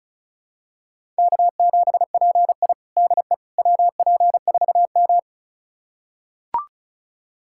Morse code, heard as a keyed tone, about 36 wpm, 700 Hz. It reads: K7PI DE WP4M
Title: USA Callsign DE Callsign - 4 Characters - 35wpm